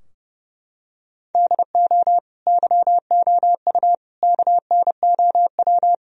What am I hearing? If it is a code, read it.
Morse code, heard as a keyed tone, about 32 wpm, 700 Hz. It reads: DO YOU KNOW
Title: Sentences from Top 1000 Words - Review Part 4 - 30wpm